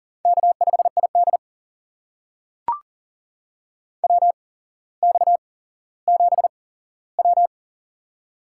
Morse code, heard as a keyed tone, about 40 wpm, spaced as for 17 wpm, 700 Hz. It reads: K5ID WX7W
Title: USA Call Signs - 4 Characters - 8x Character Spacing - 40wpm